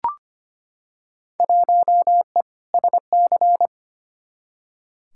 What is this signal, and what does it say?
Morse code, heard as a keyed tone, about 25 wpm, 700 Hz. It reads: 1E SC
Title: Field Day Exchanges - 25wpm